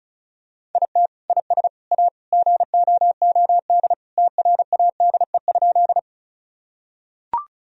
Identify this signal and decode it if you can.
Morse code, heard as a keyed tone, about 36 wpm, 700 Hz. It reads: IT IS A GOOD TRADE?
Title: Sentences from Top 1000 Words - Review Part 4 - 35wpm